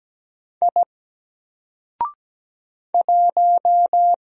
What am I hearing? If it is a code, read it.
Morse code, heard as a keyed tone, about 17 wpm, 700 Hz. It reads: I 1
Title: Single Letter-Number Alt 17wpm